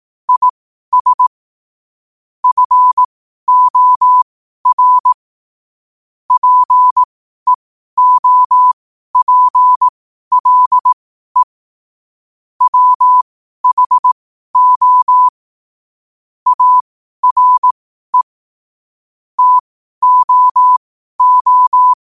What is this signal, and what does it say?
Morse code, heard as a keyed tone, about 18 wpm, spaced as for 12 wpm, 1000 Hz. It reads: IS FOR PEOPLE WHO ARE TOO
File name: qotdfiles / 12wpm